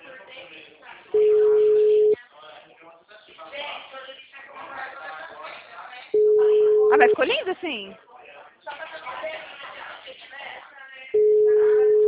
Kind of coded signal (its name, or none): telephone call progress tone